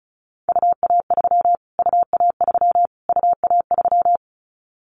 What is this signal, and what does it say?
Morse code, heard as a keyed tone, about 34 wpm, 700 Hz. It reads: UA3 UA3 UA3
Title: Callsign Prefixes for Top 25 DXCC - Repeated 3x - 35wpm